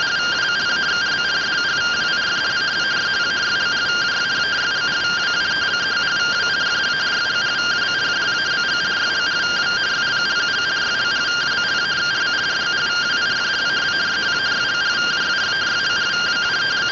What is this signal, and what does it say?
Two-tone teleprinter signal, about 170 Hz shift.
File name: findings / CT AXIAL -256.gen-21 rtty rx